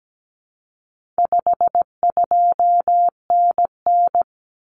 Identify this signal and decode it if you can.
Morse code, heard as a keyed tone, about 17 wpm, 700 Hz. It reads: 52NN
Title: Common Cut Numbers - 4-Digit 17wpm